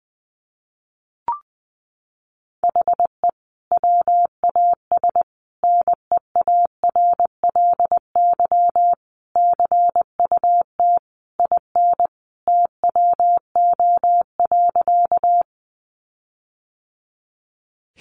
Morse code, heard as a keyed tone, about 20 wpm, 700 Hz. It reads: HE WAS NEARLY CUT IN TWO.
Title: Sentences from Top 1300 Words 20wpm